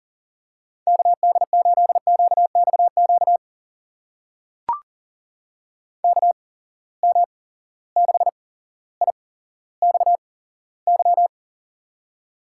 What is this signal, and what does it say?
Morse code, heard as a keyed tone, about 40 wpm, spaced as for 17 wpm, 700 Hz. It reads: KD8QXQ KM6IXY
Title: USA Call Signs - 6 Characters - 8x Character Spacing - 40wpm